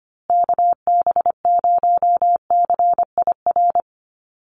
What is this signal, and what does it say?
Morse code, heard as a keyed tone, about 25 wpm, 700 Hz. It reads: KB0CIR
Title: USA Call Signs Repeated 3x 25wpm